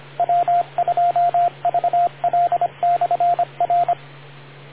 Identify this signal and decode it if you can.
Morse code, heard as a keyed tone, about 25 wpm, 700 Hz, with radio noise. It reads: W2VL/R